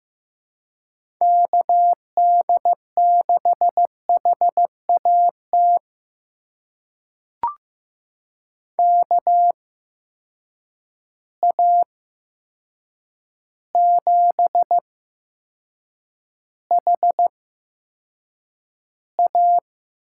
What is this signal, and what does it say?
Morse code, heard as a keyed tone, about 15 wpm, spaced as for 7 wpm, 700 Hz. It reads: KD6HAT KA7HA
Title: USA Call Signs - 6 Characters - 8x Character Spacing - 15wpm